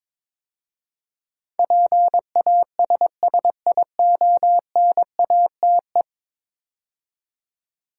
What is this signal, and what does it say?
Morse code, heard as a keyed tone, about 22 wpm, 700 Hz. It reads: PASSIONATE
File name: practice / 22wpm / 10 Letter Words - First 3 Letters Given - 22wpm